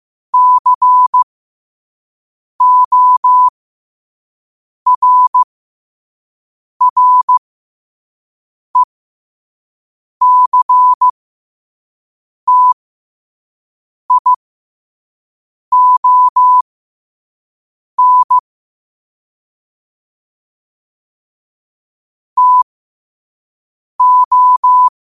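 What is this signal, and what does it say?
Morse code, heard as a keyed tone, about 15 wpm, spaced as for 5 wpm, 1000 Hz. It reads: CORRECTION TO
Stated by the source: Morse practice at 5 wpm.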